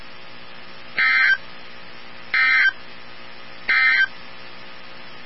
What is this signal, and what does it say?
Emergency Alert System (SAME) tones, heard as harsh buzzing bursts. It is heard from 0.9 to 4.2 s.
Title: NOAA Weather Radio 162.550